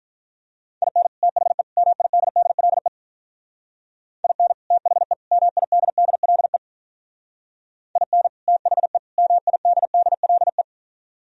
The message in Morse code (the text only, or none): IN THE MIDDLE IN THE MIDDLE IN THE MIDDLE